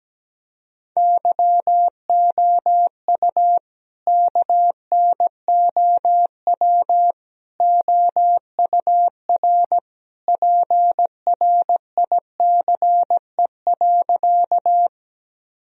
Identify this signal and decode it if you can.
Morse code, heard as a keyed tone, about 17 wpm, 700 Hz. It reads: YOU KNOW OUR PRICE.
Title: Sentences from Top 1000 Words - Review Part 4 - 17wpm